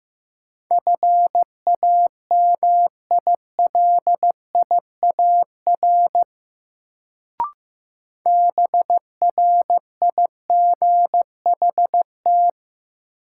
Morse code, heard as a keyed tone, about 15 wpm, 700 Hz. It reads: FAMILIAR BRIGHT
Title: Sentences from Top 1000 Words - New Word Drill - 15wpm